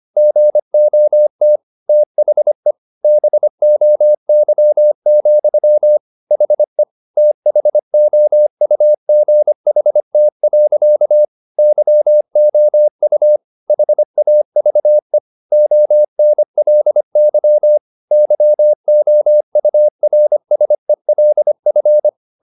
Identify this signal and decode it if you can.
Morse code, heard as a keyed tone, about 25 wpm, 600 Hz. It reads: GOT THE BOY, HE THOUGHT. YOU HAVE ONLY YOURSELF
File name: the old man and the sea part 1 25wpm